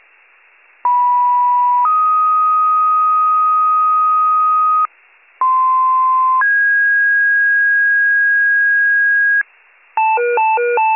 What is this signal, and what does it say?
Two-tone page, heard as a one-second tone, then a three-second tone, about 980 then 1250 Hz, about 1040 then 1650 Hz, with noise. It is heard from 0.8 to 9.4 s.